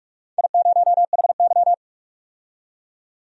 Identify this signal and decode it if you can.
Morse code, heard as a keyed tone, about 46 wpm, 700 Hz. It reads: I0HY
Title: DXCC-100 Call Signs - 4 Characters - 45wpm